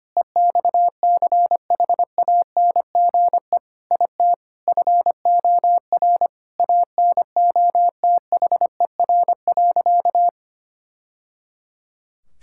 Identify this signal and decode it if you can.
Morse code, heard as a keyed tone, about 25 wpm, 700 Hz. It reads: EXCHANGE IT FOR ANOTHER.
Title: Sentences from Top 1000 Words - Review Part 6 - 25wpm